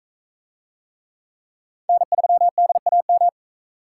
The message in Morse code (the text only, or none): N3DAM